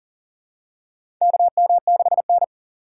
Morse code, heard as a keyed tone, about 40 wpm, 700 Hz. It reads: KM6N
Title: USA Call Signs - 4 Characters - 2x Character Spacing - 40wpm